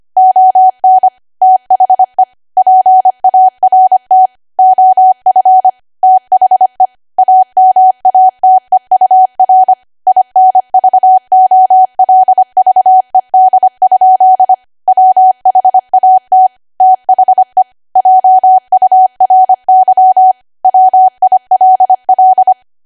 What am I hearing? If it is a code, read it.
Morse code, heard as a keyed tone, about 25 wpm, 750 Hz. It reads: ON THE PART OF THE AMATEUR INVOLVED? WHAT THE JURY WILL
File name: W1AW-Level3/140218 25WPM